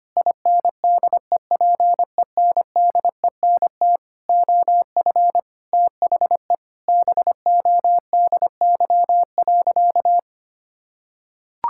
Morse code, heard as a keyed tone, about 25 wpm, 700 Hz. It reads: INDEPENDENT OF THE BODY.
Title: Sentences from Top 1000 Words - Review Part 6 - 25wpm